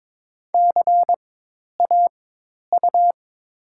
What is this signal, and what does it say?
Morse code, heard as a keyed tone, about 22 wpm, spaced as for 10 wpm, 700 Hz. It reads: CAU
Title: 4x Character Spacing - Top 200 Words - 22wpm